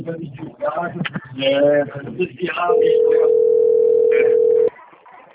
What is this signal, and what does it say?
Ringback tone (North America), 440 + 480 Hz, with speech.
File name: rg-1600-unknown- - - .14384